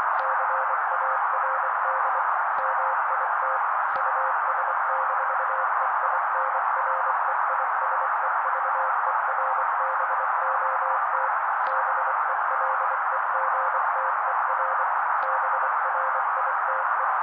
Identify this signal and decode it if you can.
Morse code, heard as a keyed tone, about 24 wpm, 530 Hz. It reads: CARD MIT US-EINREISEVERBOT BELEGTER BRITE
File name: daily / 23wpm noise